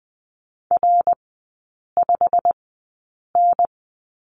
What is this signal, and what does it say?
Morse code, heard as a keyed tone, about 20 wpm, spaced as for 8 wpm, 690 Hz. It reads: R5N
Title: 087 Lesson 15e - Number 5 - Three Characters 20wpm